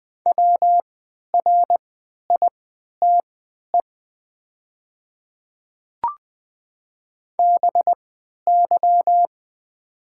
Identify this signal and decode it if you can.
Morse code, heard as a keyed tone, about 20 wpm, spaced as for 11 wpm, 700 Hz. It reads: WRITE BY
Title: 3x Character Spacing - Top 500 Words - 20wpm